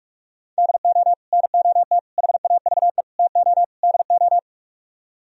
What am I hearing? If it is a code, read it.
Morse code, heard as a keyed tone, about 49 wpm, 700 Hz. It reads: DO NOT HAVE TO DO